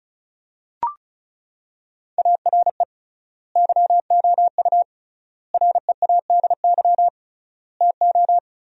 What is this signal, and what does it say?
Morse code, heard as a keyed tone, about 34 wpm, 700 Hz. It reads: ARE YOU READY TO
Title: 3x Word-Spacing - Sentences from Top 400 Words - 35wpm